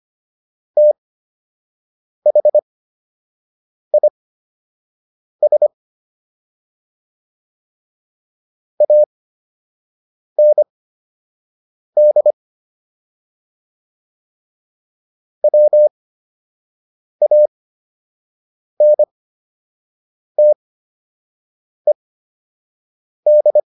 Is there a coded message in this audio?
Yes: Morse code from 0.8 to 23.6 s, heard as a keyed tone, about 25 wpm, spaced as for 6 wpm, 600 Hz. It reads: THIS AND WANTED